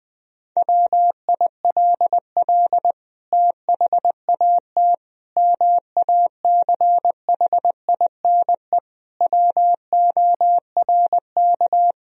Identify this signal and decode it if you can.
Morse code, heard as a keyed tone, about 21 wpm, 700 Hz. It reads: WILL THAT MACHINE WORK
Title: Sentences from Top 1000 Words - Review Part 2 - 20wpm